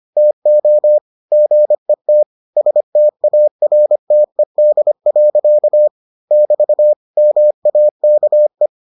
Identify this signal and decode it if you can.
Morse code, heard as a keyed tone, about 25 wpm, 600 Hz. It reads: TO GET STARTED. <BT> MAKE